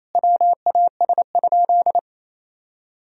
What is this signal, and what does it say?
Morse code, heard as a keyed tone, about 28 wpm, 700 Hz. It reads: WAS?
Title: Sentences from Top 1000 Words - Review Part 4 - 28wpm